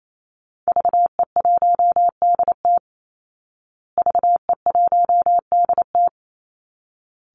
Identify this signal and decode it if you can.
Morse code, heard as a keyed tone, about 28 wpm, 700 Hz. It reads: VE1DT VE1DT
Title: Call Signs Repeated 3x - 28wpm